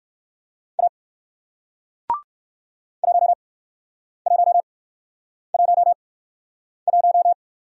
Morse code, heard as a keyed tone, about 56 wpm, spaced as for 9 wpm, 700 Hz. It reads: A 1111
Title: Speed-Racing Single Letter-Number 45wpm